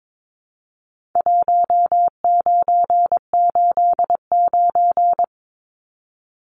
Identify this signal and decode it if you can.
Morse code, heard as a keyed tone, about 22 wpm, 700 Hz. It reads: 1989
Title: Numbers - 4-Digit 22wpm